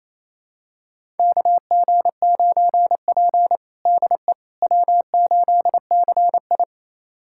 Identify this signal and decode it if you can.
Morse code, heard as a keyed tone, about 28 wpm, 700 Hz. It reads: KG9P DE W8CI
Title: USA Callsign DE Callsign - 4 Characters - 28wpm